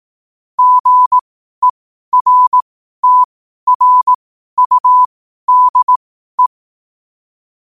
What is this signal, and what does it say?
Morse code, heard as a keyed tone, about 18 wpm, spaced as for 12 wpm, 1000 Hz. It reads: GERTRUDE
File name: qotdfiles / 12wpm